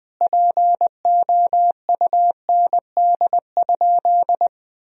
Morse code, heard as a keyed tone, about 21 wpm, 700 Hz. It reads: POUND?
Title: Sentences from Top 1000 Words - Review Part 2 - 20wpm